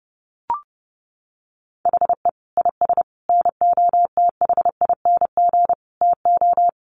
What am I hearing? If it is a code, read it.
Morse code, heard as a keyed tone, about 30 wpm, 700 Hz. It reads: HE IS NOTHING TO
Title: Sentences from Top 1200 Words 30wpm